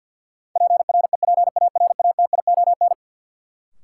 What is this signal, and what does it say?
Morse code, heard as a keyed tone, about 51 wpm, 700 Hz. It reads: PREPARATION
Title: Sentences from Top 1400 Words - New Word Drill Alt - 50wpm